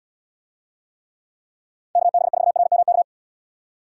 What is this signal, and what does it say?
Morse code, heard as a keyed tone, about 78 wpm, 700 Hz. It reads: KB4RMK